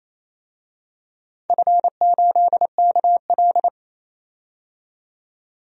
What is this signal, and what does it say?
Morse code, heard as a keyed tone, about 28 wpm, 700 Hz. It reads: F8KL
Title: DXCC-100 Call Signs - 4 Characters - 28wpm